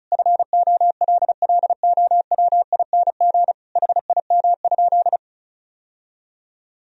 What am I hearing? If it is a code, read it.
Morse code, heard as a keyed tone, about 34 wpm, 700 Hz. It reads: FOLLOWING HIM?
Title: Sentences from Top 1000 Words - Review Part 4 - 35wpm